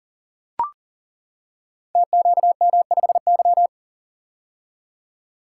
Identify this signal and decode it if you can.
Morse code, heard as a keyed tone, about 37 wpm, 700 Hz. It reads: TQM5Y
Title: Code Groups - 5 Letter:Numbers - 40wpm